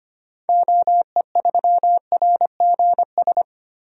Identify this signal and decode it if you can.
Morse code, heard as a keyed tone, about 25 wpm, 700 Hz. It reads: OE3RGS
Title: DXCC-100 Call Signs - 6 Characters - 25wpm